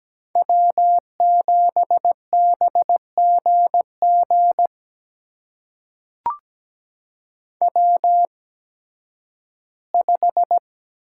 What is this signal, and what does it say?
Morse code, heard as a keyed tone, about 17 wpm, spaced as for 11 wpm, 700 Hz. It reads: W7BGG W5
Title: USA Call Signs - 8x Character Spacing - 17wpm